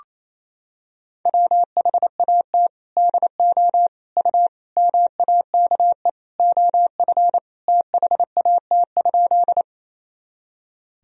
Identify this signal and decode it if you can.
Morse code, heard as a keyed tone, about 28 wpm, 700 Hz. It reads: WHAT DO U MAKE OF THAT?
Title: Sentences from Top 100 Words and Abbreviations 28wpm